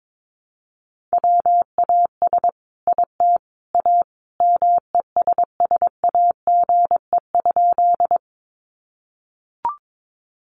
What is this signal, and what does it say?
Morse code, heard as a keyed tone, about 22 wpm, 700 Hz. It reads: WAS IT A MESSAGE?
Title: Sentences from Top 1100 Words 22wpm